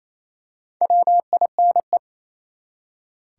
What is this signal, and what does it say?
Morse code, heard as a keyed tone, about 28 wpm, 700 Hz. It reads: WINE
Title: Four Letter Words - 28wpm